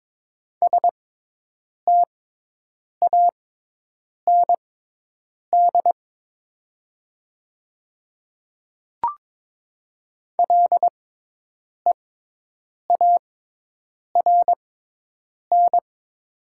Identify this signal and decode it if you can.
Morse code, heard as a keyed tone, about 23 wpm, spaced as for 8 wpm, 700 Hz. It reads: STAND LEARN
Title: 6x Character Spacing - Top 300 Words - 22wpm